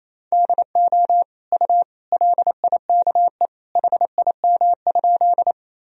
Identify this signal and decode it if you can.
Morse code, heard as a keyed tone, about 28 wpm, 700 Hz. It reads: DO U LIKE HIM?
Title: Sentences from Top 100 Words and Abbreviations 28wpm